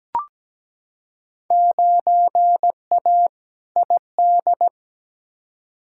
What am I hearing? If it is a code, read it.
Morse code, heard as a keyed tone, about 17 wpm, 700 Hz. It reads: 9A ID
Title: Field Day Exchanges Encore - 17wpm